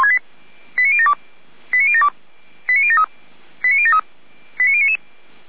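Five-tone selective call (ZVEI), heard as quick runs of five tones, tones of about 70 ms. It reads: 89072 89072 89073 89073 89099
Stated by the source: ZVEI five-tone selective call 89170, received with an SDR.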